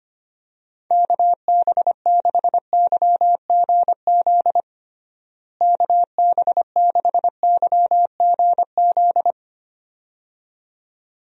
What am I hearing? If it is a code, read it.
Morse code, heard as a keyed tone, about 25 wpm, 700 Hz. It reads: KB6YGZ KB6YGZ
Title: USA Call Signs - 6 Characters - Repeated 2x - 25wpm